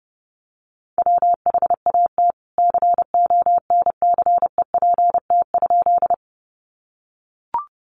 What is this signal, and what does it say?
Morse code, heard as a keyed tone, about 30 wpm, 700 Hz. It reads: WHAT CONCEPT?
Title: Sentences from Top 1100 Words 30wpm